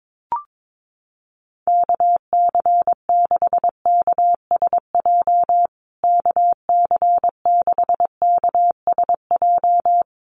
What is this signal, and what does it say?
Morse code, heard as a keyed tone, about 22 wpm, 700 Hz. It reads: KC6KSJ KC6KSJ
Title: USA Call Signs Repeated 3x 22wpm